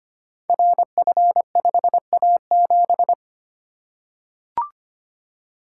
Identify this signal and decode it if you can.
Morse code, heard as a keyed tone, about 25 wpm, 700 Hz. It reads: RF5A7
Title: Code Groups - 5 Letter:Numbers - 25wpm